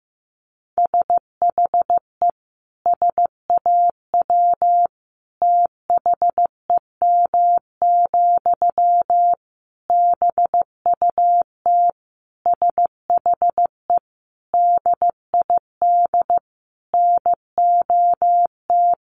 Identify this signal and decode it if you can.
Morse code, heard as a keyed tone, about 14 wpm, 700 Hz. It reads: SHE SAW THEM, BUT SHE DID NOT
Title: Sentences from Top 1000 Words 15wpm